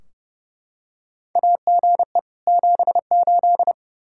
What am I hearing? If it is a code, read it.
Morse code, heard as a keyed tone, about 30 wpm, 700 Hz. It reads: AGE 78
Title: Practice QSO - Exchange 3 - More Info - 30wpm